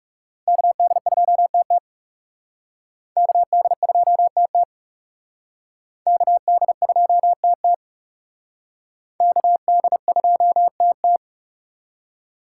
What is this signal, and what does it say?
Morse code, heard as a keyed tone, about 36 wpm, 700 Hz. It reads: KD2TT KD2TT KD2TT KD2TT
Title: Speed Racing - USA Call Signs 30wpm